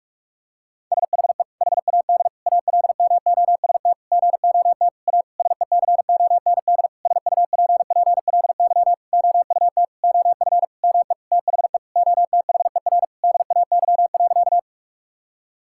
Morse code, heard as a keyed tone, about 41 wpm, 700 Hz. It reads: SHE HAD ALMOST GOT A SECOND SUPPLY OUT OF ME THE OTHER DAY.
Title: Sentences from Top 1000 Words - Review Part 5 - 45wpm